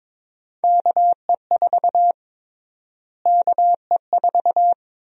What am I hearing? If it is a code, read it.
Morse code, heard as a keyed tone, about 22 wpm, 700 Hz. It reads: KE4 KE4
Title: USA Call Sign Prefixes - Repeated 2x - 22wpm